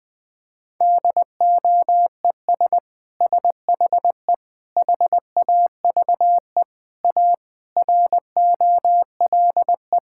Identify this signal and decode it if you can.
Morse code, heard as a keyed tone, about 20 wpm, 700 Hz. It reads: DOES SHE HAVE A ROLE
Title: Sentences from Top 1000 Words - Review Part 6 - 20wpm